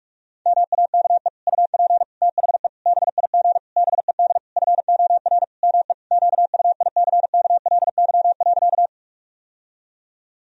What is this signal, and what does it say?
Morse code, heard as a keyed tone, about 46 wpm, 700 Hz. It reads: MAKE UP THE BIG BED FOR ME QUICKLY.
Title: Sentences from Top 1000 Words - Review Part 1 - 45wpm